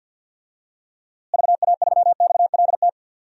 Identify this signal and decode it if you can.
Morse code, heard as a keyed tone, about 50 wpm, 700 Hz. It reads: VA3XLT
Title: DXCC-100 Call Signs - 6 Characters - Repeated 3x - 50wpm